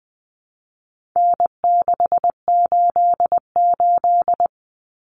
Morse code, heard as a keyed tone, about 20 wpm, 700 Hz. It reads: N688
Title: Common Cut Numbers - 4-Digit 20wpm